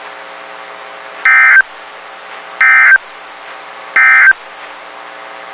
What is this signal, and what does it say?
Emergency Alert System (SAME) tones, heard as harsh buzzing bursts. It is heard from 1.1 to 4.4 s.
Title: WXK91 NOAA Weather Radio